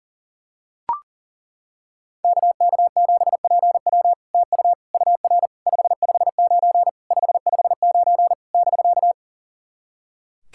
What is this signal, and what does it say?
Morse code, heard as a keyed tone, about 40 wpm, 700 Hz. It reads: KK7PW TU UR 559 559 <BK>
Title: SOTA QSO Practice 40wpm